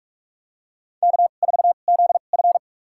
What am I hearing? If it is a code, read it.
Morse code, heard as a keyed tone, about 46 wpm, spaced as for 32 wpm, 700 Hz. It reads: K4ZF
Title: USA Call Signs - 2x Character Spacing - 45wpm